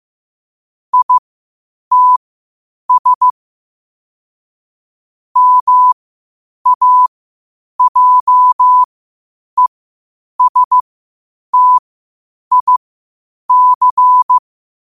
Morse code, heard as a keyed tone, about 15 wpm, spaced as for 8 wpm, 1000 Hz. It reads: ITS MAJESTIC
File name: qotdfiles / 8wpm